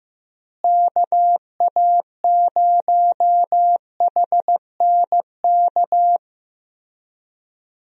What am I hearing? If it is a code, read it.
Morse code, heard as a keyed tone, about 15 wpm, 700 Hz. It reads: KA0HNK
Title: USA Call Signs - 6 Characters - Repeated 3x - 15wpm